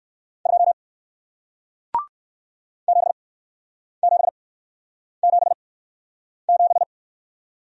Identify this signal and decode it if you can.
Morse code, heard as a keyed tone, about 60 wpm, spaced as for 9 wpm, 700 Hz. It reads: 2 7777